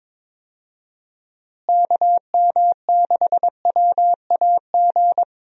Morse code, heard as a keyed tone, about 22 wpm, 700 Hz. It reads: KM6WAG
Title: USA Call Signs - 6 Characters - Repeated 3x - 22wpm